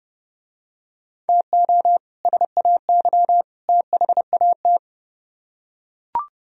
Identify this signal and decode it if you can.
Morse code, heard as a keyed tone, about 29 wpm, 700 Hz. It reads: TO SAY THAT